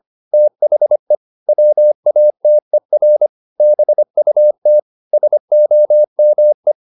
Morse code, heard as a keyed tone, about 25 wpm, 600 Hz. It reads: THE WATER BUT SOME
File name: the old man and the sea part 1 25wpm